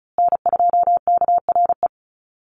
Morse code, heard as a keyed tone, about 34 wpm, 700 Hz. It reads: N2KRE
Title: USA Call Signs Repeated 2x 35wpm